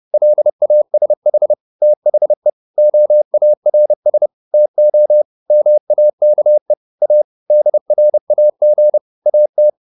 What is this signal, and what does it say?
Morse code, heard as a keyed tone, about 30 wpm, 600 Hz. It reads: LASH THE OARS TO MAKE A DRAG AT